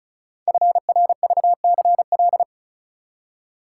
Morse code, heard as a keyed tone, about 36 wpm, 700 Hz. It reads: FRVCL
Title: Code Groups - 5 Letters - 35wpm